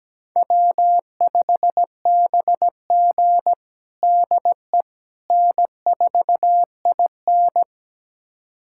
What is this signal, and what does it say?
Morse code, heard as a keyed tone, about 17 wpm, 700 Hz. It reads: W5BG DE N4IN
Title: USA Callsign DE Callsign - 4 Characters - 17wpm